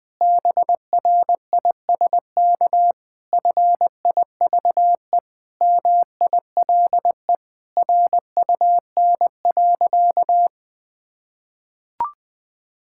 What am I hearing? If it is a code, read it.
Morse code, heard as a keyed tone, about 20 wpm, 700 Hz. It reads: BRISK FIVE MILE RUN.
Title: Difficult Sentences with Hints - 20wpm